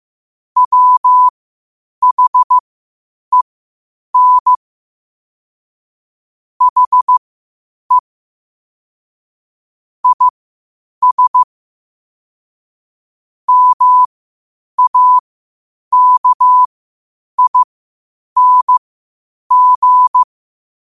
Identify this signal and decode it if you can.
Morse code, heard as a keyed tone, about 15 wpm, spaced as for 8 wpm, 1000 Hz. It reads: WHEN HE IS MAKING
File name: qotdfiles / 8wpm